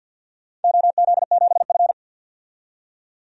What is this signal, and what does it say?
Morse code, heard as a keyed tone, about 51 wpm, 700 Hz. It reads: OZ7F
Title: DXCC-100 Call Signs - 4 Characters - 50wpm